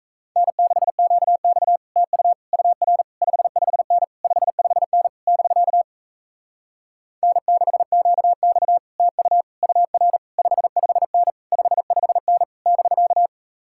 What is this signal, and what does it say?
Morse code, heard as a keyed tone, about 42 wpm, 700 Hz. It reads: N6QX TU UR 55N 55N <BK> N6QX TU UR 55N 55N <BK>